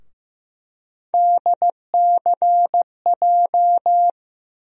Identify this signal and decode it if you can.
Morse code, heard as a keyed tone, about 15 wpm, 700 Hz. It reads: DCJ
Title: USA Call Sign Postfixes - 3 Letters 15wpm